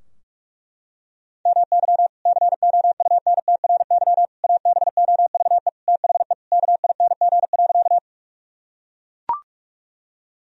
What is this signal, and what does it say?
Morse code, heard as a keyed tone, about 42 wpm, 700 Hz. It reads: MY COUNTRY ABOVE THE KING.
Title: Sentences from Top 1000 Words - Review Part 2 - 45wpm